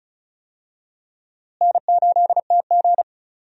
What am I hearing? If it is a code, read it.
Morse code, heard as a keyed tone, about 34 wpm, 700 Hz. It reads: N8TG
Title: USA Call Signs - 4 Characters - Repeated 3x - 35wpm